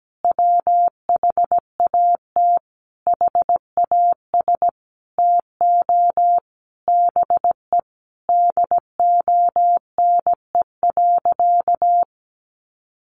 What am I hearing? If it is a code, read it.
Morse code, heard as a keyed tone, about 17 wpm, 700 Hz. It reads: WHAT HAS TO BE DONE.
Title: Sentences from Top 700 Words 17wpm